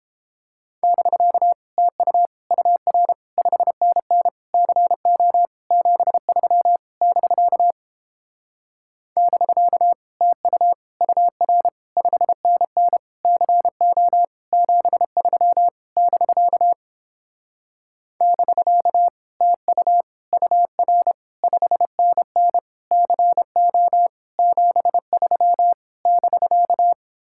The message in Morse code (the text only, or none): <BK> TU UR 5NN CO 73 <BK> <BK> TU UR 5NN CO 73 <BK> <BK> TU UR 5NN CO 73 <BK>